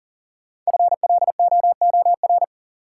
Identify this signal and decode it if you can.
Morse code, heard as a keyed tone, about 40 wpm, 700 Hz. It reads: FLOOR
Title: Sets of 3 Words - New Word Drill - 40wpm